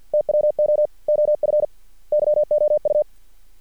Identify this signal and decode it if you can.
Morse code, heard as a keyed tone, about 50 wpm, 600 Hz. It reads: TWO OF YOU
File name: archived / CRSnet